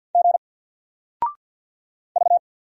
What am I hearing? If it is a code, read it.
Morse code, heard as a keyed tone, about 53 wpm, 700 Hz. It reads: G V
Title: Single Letter-Number Alt 50wpm